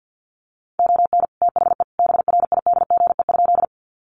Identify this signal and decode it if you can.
Morse code, heard as a keyed tone, about 50 wpm, 700 Hz. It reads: ON THE BRIDGE?
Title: Sentences from Top 1200 Words 50wpm